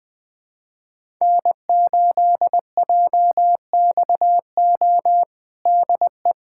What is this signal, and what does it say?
Morse code, heard as a keyed tone, about 20 wpm, 700 Hz. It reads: N8JXO DE
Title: USA Callsign DE Callsign 20wpm